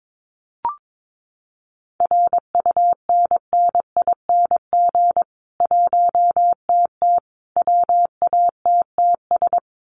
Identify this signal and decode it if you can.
Morse code, heard as a keyed tone, about 22 wpm, 700 Hz. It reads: RUNNING 1TT WATTS
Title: Common QSO Phrases 22wpm